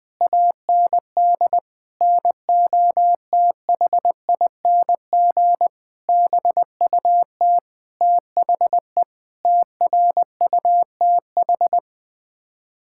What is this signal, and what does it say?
Morse code, heard as a keyed tone, about 20 wpm, 700 Hz. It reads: AND NOTHING BUT THE TRUTH
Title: Sets of 8 Words - 20wpm